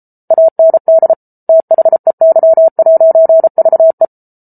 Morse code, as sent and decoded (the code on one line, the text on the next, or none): .- -. -.. / - .... -.-- .----. ...- .
AND THEY'VE